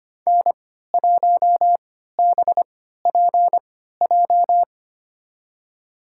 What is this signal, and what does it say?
Morse code, heard as a keyed tone, about 25 wpm, spaced as for 14 wpm, 700 Hz. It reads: N1BPJ
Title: USA Call Signs - 5 Characters - 3x Character Spacing - 25wpm